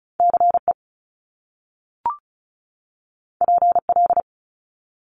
Morse code, heard as a keyed tone, about 36 wpm, 700 Hz. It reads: CE PL
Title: Top Two Letter Combinations in Words 35wpm